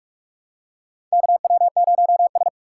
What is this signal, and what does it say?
Morse code, heard as a keyed tone, about 46 wpm, 700 Hz. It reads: KW0S
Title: USA Call Signs - 4 Characters - 2x Character Spacing - 45wpm